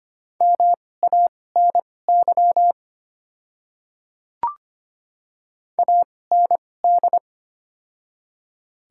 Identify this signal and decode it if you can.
Morse code, heard as a keyed tone, about 25 wpm, spaced as for 18 wpm, 690 Hz. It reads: MANY AND
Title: 2x Character Spacing - Top 200 Words - 25wpm